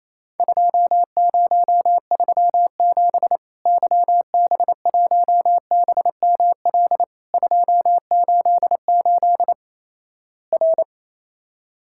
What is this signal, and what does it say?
Morse code, heard as a keyed tone, about 28 wpm, 700 Hz. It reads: 2037 YB1BML 288 R
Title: Worked All Europe DX Contest - QTC Exchange - 28wpm